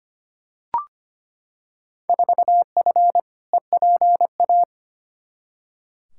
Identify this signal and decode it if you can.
Morse code, heard as a keyed tone, about 25 wpm, 700 Hz. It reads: 4F EPA
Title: Field Day Exchanges - 25wpm